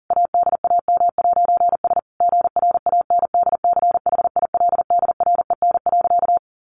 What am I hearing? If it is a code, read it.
Morse code, heard as a keyed tone, about 40 wpm, 700 Hz. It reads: ADAM'S GRANDCHILDREN.
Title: A Sound Of Thunder 40wpm